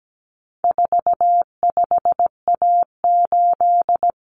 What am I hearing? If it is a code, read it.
Morse code, heard as a keyed tone, about 17 wpm, 700 Hz. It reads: 45A8
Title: Common Cut Numbers - 4-Digit 17wpm